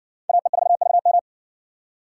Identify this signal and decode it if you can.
Morse code, heard as a keyed tone, about 54 wpm, 700 Hz. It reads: AE4VM